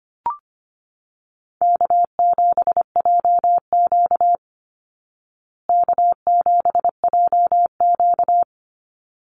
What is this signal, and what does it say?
Morse code, heard as a keyed tone, about 25 wpm, 700 Hz. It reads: K7JQ K7JQ
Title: Call Signs Repeated 3x - 25wpm